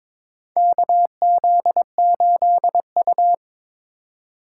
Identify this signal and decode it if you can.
Morse code, heard as a keyed tone, about 22 wpm, 700 Hz. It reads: KZ8U
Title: USA Call Signs - 4 Characters - 4x Character Spacing - 22wpm